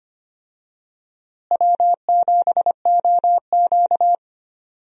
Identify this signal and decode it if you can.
Morse code, heard as a keyed tone, about 25 wpm, 700 Hz. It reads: W7OQ
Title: USA Call Signs - 4 Characters - Repeated 3x - 25wpm